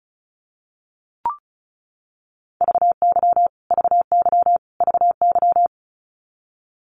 Morse code, heard as a keyed tone, about 34 wpm, 700 Hz. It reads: VY VY VY
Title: Abbreviations Repeated 3x - 35wpm